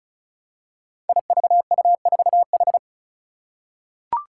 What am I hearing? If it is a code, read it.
Morse code, heard as a keyed tone, about 36 wpm, 700 Hz. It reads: IVU4H
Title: Five Challenging Letter-Number Combinations 35wpm